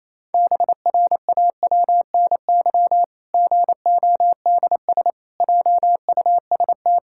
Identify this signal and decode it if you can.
Morse code, heard as a keyed tone, about 28 wpm, 700 Hz. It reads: BRAWNY GODS JUST
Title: Pangram Challenge v2 - 28wpm